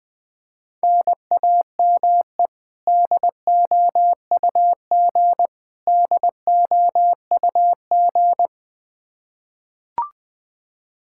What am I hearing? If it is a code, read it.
Morse code, heard as a keyed tone, about 20 wpm, 700 Hz. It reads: NAME DOUG DOUG
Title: Practice QSO - Exchange 1 - Beginning - 20wpm